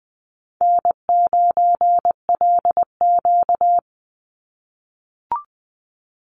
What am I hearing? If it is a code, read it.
Morse code, heard as a keyed tone, about 20 wpm, 700 Hz. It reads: N9LQ
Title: Easy Call Signs - 20wpm